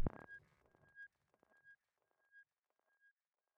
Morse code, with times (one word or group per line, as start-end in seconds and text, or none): none